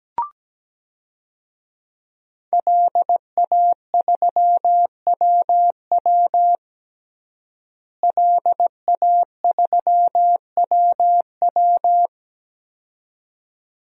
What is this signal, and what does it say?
Morse code, heard as a keyed tone, about 17 wpm, 700 Hz. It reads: LA3WW LA3WW
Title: DXCC-100 Call Signs - 5 Characters - Repeated 2x - 17wpm